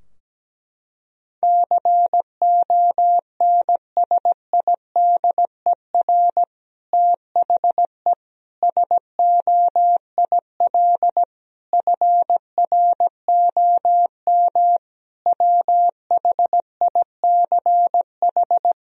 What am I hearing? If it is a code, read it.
Morse code, heard as a keyed tone, about 17 wpm, 700 Hz. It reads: CONSIDER THE SOIL FROM WHICH
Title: Sentences from Top 1000 Words - Review Part 4 - 17wpm